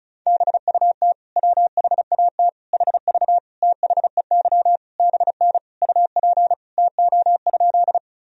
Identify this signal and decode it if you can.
Morse code, heard as a keyed tone, about 36 wpm, 700 Hz. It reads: BUT WHAT HV THEY BN UP TO?
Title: Sentences from Top 100 Words and Abbreviations 35wpm